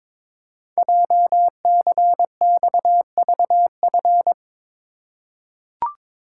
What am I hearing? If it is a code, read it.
Morse code, heard as a keyed tone, about 22 wpm, 700 Hz. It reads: JCXVF
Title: Code Groups - 5 Letters - 22wpm